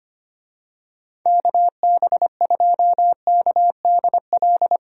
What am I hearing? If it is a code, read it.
Morse code, heard as a keyed tone, about 25 wpm, 700 Hz. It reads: KB2KDL
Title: USA Call Signs - 6 Characters - 25wpm